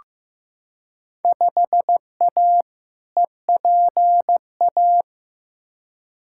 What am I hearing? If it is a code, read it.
Morse code, heard as a keyed tone, about 16 wpm, 700 Hz. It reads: 5A EPA
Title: Field Day Exchanges - 15wpm